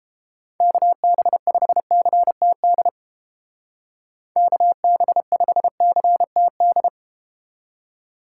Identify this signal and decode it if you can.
Morse code, heard as a keyed tone, about 31 wpm, 700 Hz. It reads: KB5CTD KB5CTD